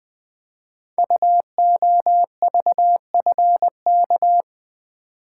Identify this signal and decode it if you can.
Morse code, heard as a keyed tone, about 20 wpm, 700 Hz. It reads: UOVFK
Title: Code Groups - 5 Letters - 20wpm